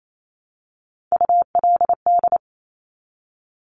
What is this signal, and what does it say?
Morse code, heard as a keyed tone, about 28 wpm, 690 Hz. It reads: ULD
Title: Top Three Letter Combinations in Words 28wpm